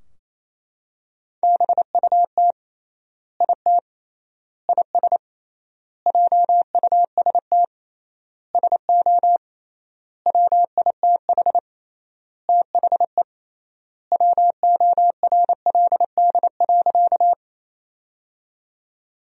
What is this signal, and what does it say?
Morse code, heard as a keyed tone, about 28 wpm, 700 Hz. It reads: BUT IT IS JUST SO WITH THE WORLD.
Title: 3x Word-Spacing - Sentences from Top 200 Words - 28wpm